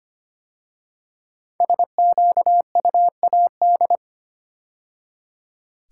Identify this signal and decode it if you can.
Morse code, heard as a keyed tone, about 25 wpm, 700 Hz. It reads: SQUAD
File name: 5 Letter Words - First 3 Letters Given - 25wpm